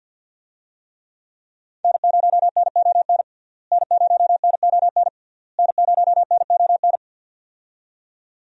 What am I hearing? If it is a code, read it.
Morse code, heard as a keyed tone, about 45 wpm, 700 Hz. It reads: N0NON N0NON N0NON
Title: USA Call Signs - 5 Characters - Repeated 3x - 50wpm